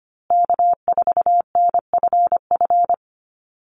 Morse code, heard as a keyed tone, about 25 wpm, 700 Hz. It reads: K4NFF
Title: USA Call Signs Repeated 2x 25wpm